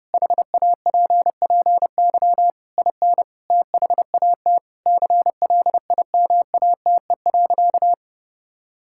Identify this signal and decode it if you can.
Morse code, heard as a keyed tone, about 30 wpm, 700 Hz. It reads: HAPPY IN THAT CLIMATE.
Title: Sentences from Top 1600 Words 30wpm